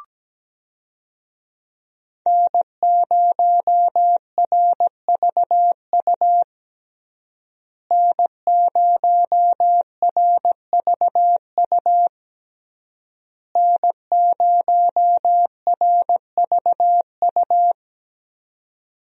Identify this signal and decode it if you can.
Morse code, heard as a keyed tone, about 17 wpm, 700 Hz. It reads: N0RVU N0RVU N0RVU
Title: USA Call Signs - 5 Characters - Repeated 3x - 17wpm